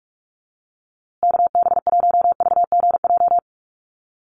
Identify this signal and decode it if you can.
Morse code, heard as a keyed tone, about 44 wpm, 700 Hz. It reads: KB1VGJ
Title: USA Call Signs Repeated 2x 45wpm